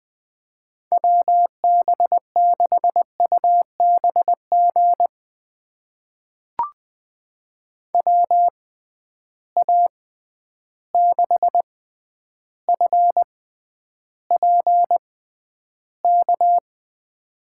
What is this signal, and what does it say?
Morse code, heard as a keyed tone, about 19 wpm, spaced as for 10 wpm, 700 Hz. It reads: WB6UBG WA6FPK
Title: USA Call Signs - 6 Characters - 6x Character Spacing - 20wpm